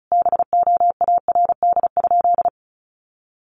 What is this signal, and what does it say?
Morse code, heard as a keyed tone, about 34 wpm, 700 Hz. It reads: BOARD?
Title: Sentences from Top 1300 Words 35wpm